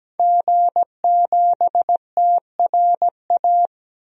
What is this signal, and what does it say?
Morse code, heard as a keyed tone, about 17 wpm, 700 Hz. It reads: G7TRA
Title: DXCC-100 Call Signs - 5 Characters - 17wpm